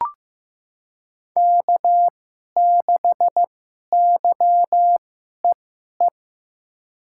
Morse code, heard as a keyed tone, about 16 wpm, spaced as for 11 wpm, 700 Hz. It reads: K6YEE
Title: USA Call Signs - 2x Character Spacing - 15wpm